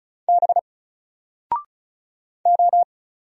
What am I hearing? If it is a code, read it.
Morse code, heard as a keyed tone, about 35 wpm, 700 Hz. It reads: B O
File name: Single-Letter-Number-35wpm